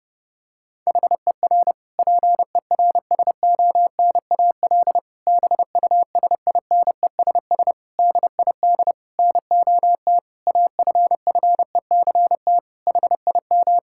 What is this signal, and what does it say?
Morse code, heard as a keyed tone, about 32 wpm, 700 Hz. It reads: HER PERSONAL BUSINESS DID NOT AFFECT HIM
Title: Sentences from Top 1000 Words - Review Part 6 - 30wpm